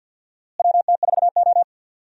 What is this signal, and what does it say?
Morse code, heard as a keyed tone, about 47 wpm, 700 Hz. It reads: WT4O
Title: USA Call Signs - 4 Characters - Repeated 2x - 50wpm